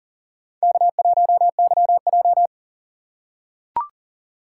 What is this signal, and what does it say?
Morse code, heard as a keyed tone, about 40 wpm, 700 Hz. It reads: K1YJ
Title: USA Call Signs - 4 Characters - 2x Character Spacing - 40wpm